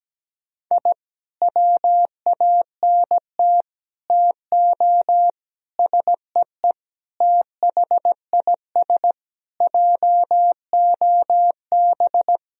Morse code, heard as a keyed tone, about 17 wpm, 700 Hz. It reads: I WANT TO SEE THIS JOB
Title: Sentences from Top 1000 Words - Review Part 3 - 17wpm